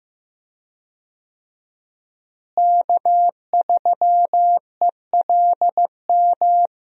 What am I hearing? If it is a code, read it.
Morse code, heard as a keyed tone, about 15 wpm, 700 Hz. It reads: K3ELM
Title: USA Call Signs - 5 Characters - Repeated 2x - 15wpm